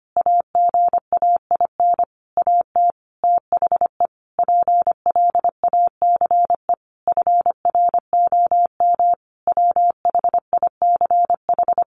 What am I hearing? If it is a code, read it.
Morse code, heard as a keyed tone, about 25 wpm, 700 Hz. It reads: AGAIN AT THE PLACE FROM WHICH
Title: Sentences from Top 1500 Words 25wpm